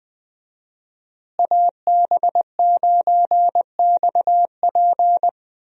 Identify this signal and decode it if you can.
Morse code, heard as a keyed tone, about 20 wpm, 700 Hz. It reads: AB9XP
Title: USA Call Signs - 5 Characters - Repeated 2x - 20wpm